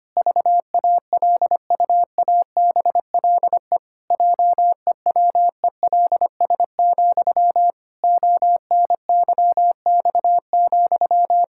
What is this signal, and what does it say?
Morse code, heard as a keyed tone, about 25 wpm, 700 Hz. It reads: VALUABLE JEWELS, ONYX,
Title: Pangram Challenge v2 - 25wpm